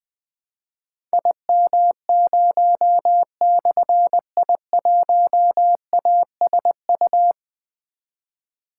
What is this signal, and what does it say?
Morse code, heard as a keyed tone, about 20 wpm, 700 Hz. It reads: IM0/I1ASU
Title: Difficult Call Signs 20wpm